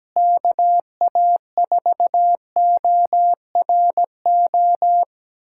Morse code, heard as a keyed tone, about 17 wpm, 700 Hz. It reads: KA4ORO
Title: USA Callsign DE Callsign 17wpm